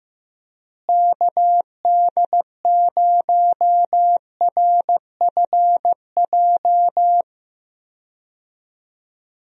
Morse code, heard as a keyed tone, about 15 wpm, 700 Hz. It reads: KD0RFJ
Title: USA Call Signs - 6 Characters - Repeated 3x - 15wpm